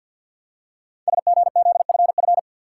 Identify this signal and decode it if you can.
Morse code, heard as a keyed tone, about 48 wpm, 700 Hz. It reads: SGZFF
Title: Code Groups - 5 Letters - 50wpm